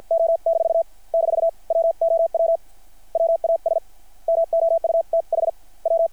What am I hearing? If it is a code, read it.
Morse code, heard as a keyed tone, about 53 wpm, spaced as for 41 wpm, 660 Hz. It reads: O-- WOW WAS MOUTH W
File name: archived / CRSnet